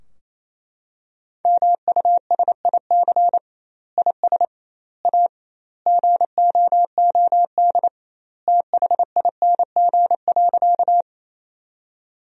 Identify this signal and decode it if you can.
Morse code, heard as a keyed tone, about 28 wpm, 700 Hz. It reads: MUSIC IS A GOOD THING.
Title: 2x Word-Spacing - Sentences from Top 300 Words - 28wpm